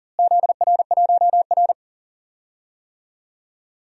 Morse code, heard as a keyed tone, about 38 wpm, 700 Hz. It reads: ZR1R
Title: DXCC-100 Call Signs - 4 Characters - Repeated 3x - 40wpm